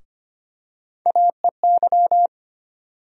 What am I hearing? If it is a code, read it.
Morse code, heard as a keyed tone, about 25 wpm, 700 Hz. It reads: AEY
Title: USA Call Sign Postfixes - 3 Letters 25wpm